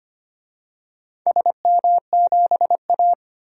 Morse code, heard as a keyed tone, about 25 wpm, 700 Hz. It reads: SM7A
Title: DXCC-100 Call Signs - 4 Characters - 25wpm